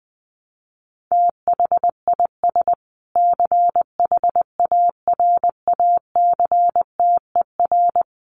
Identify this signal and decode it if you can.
Morse code, heard as a keyed tone, about 20 wpm, 700 Hz. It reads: THIS CHARACTER
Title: Sets of 5 Words Encore2 20wpm